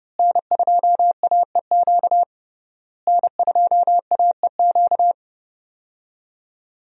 Morse code, heard as a keyed tone, about 32 wpm, 700 Hz. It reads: N2AEQ N2AEQ
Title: USA Call Signs - 5 Characters - Repeated 3x - 30wpm